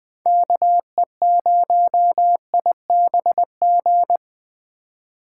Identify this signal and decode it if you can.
Morse code, heard as a keyed tone, about 20 wpm, 700 Hz. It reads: KE0IBG